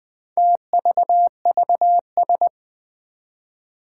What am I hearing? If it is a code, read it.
Morse code, heard as a keyed tone, about 20 wpm, 700 Hz. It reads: TVVS
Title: Four Challenging Letter-Number Combinations 20wpm